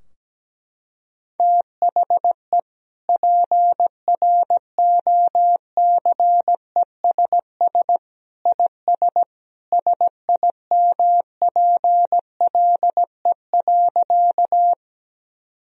Morse code, heard as a keyed tone, about 17 wpm, 700 Hz. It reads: THE PROCESS IS SIMPLE.
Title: Sentences from Top 1000 Words - Review Part 3 - 17wpm